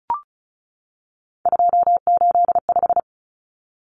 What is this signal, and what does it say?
Morse code, heard as a keyed tone, about 34 wpm, 700 Hz. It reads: 285
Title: Common Cut Numbers - 3-Digit 35wpm